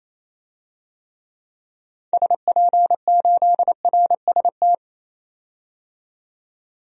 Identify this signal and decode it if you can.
Morse code, heard as a keyed tone, about 28 wpm, 700 Hz. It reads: SP8RST
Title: DXCC-100 Call Signs - 6 Characters - 28wpm